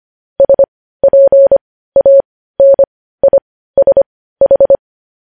Morse code, heard as a keyed tone, about 25 wpm, spaced as for 15 wpm, 550 Hz. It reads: SPANISH
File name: podcast / News headlines-1x-25x15